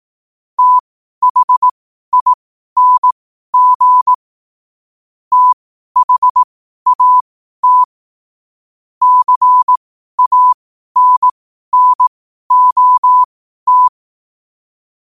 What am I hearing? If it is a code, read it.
Morse code, heard as a keyed tone, about 18 wpm, spaced as for 12 wpm, 1000 Hz. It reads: THING THAT CANNOT
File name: qotdfiles / 12wpm